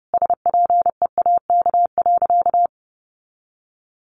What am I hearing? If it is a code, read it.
Morse code, heard as a keyed tone, about 30 wpm, 700 Hz. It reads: SPEAK.
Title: Sentences from Top 1100 Words 30wpm